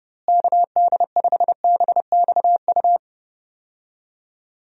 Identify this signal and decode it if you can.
Morse code, heard as a keyed tone, about 30 wpm, 700 Hz. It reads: KD5BXU
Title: USA Call Signs - 6 Characters - Repeated 3x - 30wpm